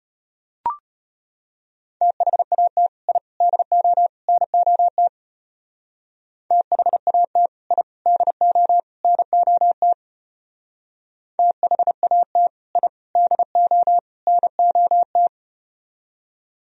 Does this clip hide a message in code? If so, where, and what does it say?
Yes: Morse code from 2.0 to 15.3 s, heard as a keyed tone, about 33 wpm, 700 Hz. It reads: THAT I DO NOT THAT I DO NOT THAT I DO NOT